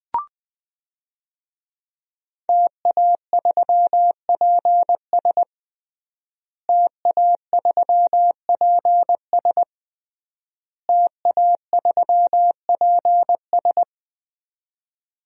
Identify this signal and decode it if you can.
Morse code, heard as a keyed tone, about 20 wpm, 700 Hz. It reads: TA3PS TA3PS TA3PS
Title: DXCC-100 Call Signs - 5 Characters - Repeated 3x - 20wpm